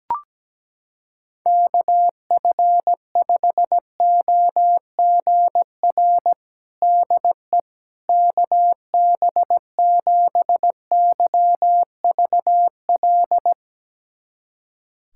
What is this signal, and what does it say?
Morse code, heard as a keyed tone, about 17 wpm, 700 Hz. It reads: KF5OGR DE KB7YVL
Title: USA Callsign DE Callsign - 6 Characters - 17wpm